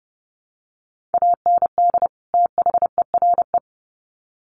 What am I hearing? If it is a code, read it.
Morse code, heard as a keyed tone, about 30 wpm, 700 Hz. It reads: AND THERE
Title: Sets of 2 Words 30wpm